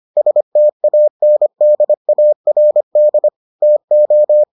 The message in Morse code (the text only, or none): STANDARD TO